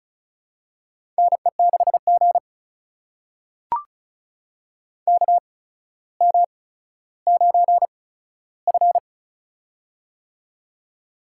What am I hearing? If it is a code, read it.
Morse code, heard as a keyed tone, about 36 wpm, spaced as for 15 wpm, 700 Hz. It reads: NE6G KM9F
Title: USA Call Signs - 4 Characters - 8x Character Spacing - 35wpm